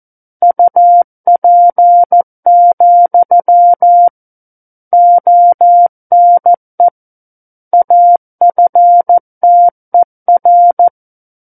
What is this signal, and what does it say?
Morse code, heard as a keyed tone, about 14 wpm, 700 Hz. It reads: UP, ONE AFTER
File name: Book One / 05-Chapter 5 - 14 WPM